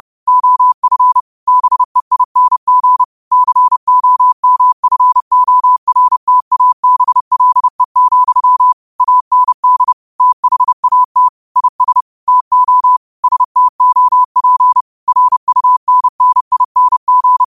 Morse code, heard as a keyed tone, about 30 wpm, 1000 Hz. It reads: OF BEING COMFORTABLE, AND THAT IS TO STOP RUNNING
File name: qotdfiles / 30wpm